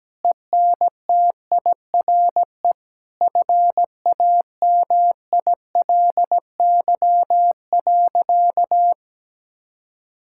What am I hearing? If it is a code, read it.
Morse code, heard as a keyed tone, about 17 wpm, 700 Hz. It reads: ENTIRE FAMILY.
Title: Sentences from Top 1000 Words - Review Part 5 - 17wpm